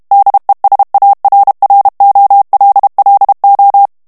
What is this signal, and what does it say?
Morse code, heard as a keyed tone, about 32 wpm, 800 Hz. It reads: DESARROLLO
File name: ESPAGNOL TEXTE EN CLAIR 2 VITESSE 30 MOTS MINUTES